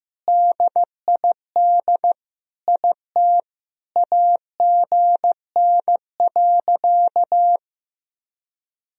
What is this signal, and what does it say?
Morse code, heard as a keyed tone, about 16 wpm, 700 Hz. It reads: DID IT AGN.
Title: Sentences from Top 300 Words and Abbreviations 15wpm